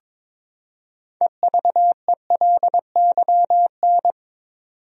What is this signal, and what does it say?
Morse code, heard as a keyed tone, about 23 wpm, 700 Hz. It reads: EVELYN
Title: USA Common Names - 22wpm